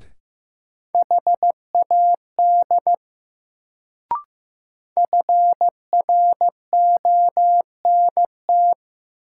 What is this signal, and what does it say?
Morse code, heard as a keyed tone, about 15 wpm, 700 Hz. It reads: HAD FRONT
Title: Sets of 2 Words - New Word Drill - 15wpm